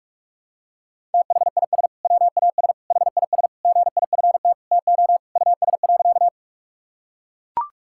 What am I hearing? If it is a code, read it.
Morse code, heard as a keyed tone, about 41 wpm, 700 Hz. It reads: THIS WAS HIS GIFT TO US.
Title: Sentences from Top 1000 Words - Review Part 6 - 45wpm